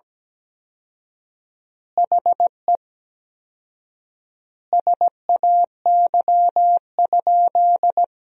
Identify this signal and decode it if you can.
Morse code, heard as a keyed tone, about 17 wpm, 700 Hz. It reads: HE SAY?
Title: 4x Word-Spacing - Sentences from Top 200 Words - 17wpm